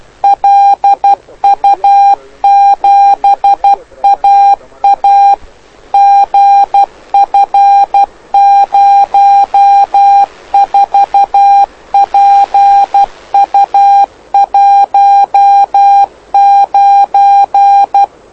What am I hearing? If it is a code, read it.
Morse code, heard as a keyed tone, about 12 wpm, 800 Hz. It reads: LU7AA GF04PU19
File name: lu4aao / lu7aa 09dic2011 exp planeador vuelo1